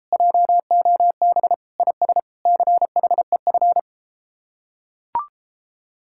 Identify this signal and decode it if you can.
Morse code, heard as a keyed tone, about 33 wpm, 700 Hz. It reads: JOB IS CHEF